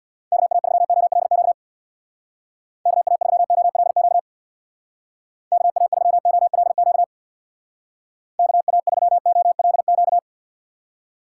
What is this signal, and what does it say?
Morse code, heard as a keyed tone, about 64 wpm, 700 Hz. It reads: KA3OLQ KA3OLQ KA3OLQ KA3OLQ